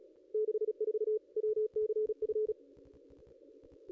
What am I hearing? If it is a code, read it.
Morse code, heard as a keyed tone, about 34 wpm, 430 Hz. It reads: 64 WCF
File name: contests / ss-cw-k5tr-op-n5rz -0500